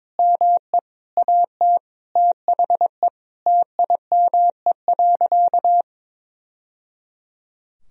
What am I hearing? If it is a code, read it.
Morse code, heard as a keyed tone, about 22 wpm, 700 Hz. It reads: ME AT THE TIME.
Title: Sentences from Top 1000 Words - Review Part 4 - 22wpm